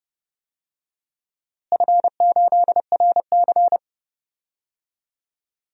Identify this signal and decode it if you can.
Morse code, heard as a keyed tone, about 31 wpm, 700 Hz. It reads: F8RC
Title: DXCC-100 Call Signs - 4 Characters - Repeated 2x - 30wpm